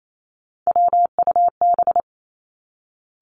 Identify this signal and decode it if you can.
Morse code, heard as a keyed tone, about 28 wpm, 700 Hz. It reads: WUB
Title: Three Challenging Letter-Number Combinations 28wpm